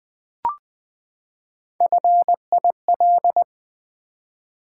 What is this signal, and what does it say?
Morse code, heard as a keyed tone, about 20 wpm, 700 Hz. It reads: FIL
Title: Build A Word - 3 Letters At A Time - 9-Letter Words - 20wpm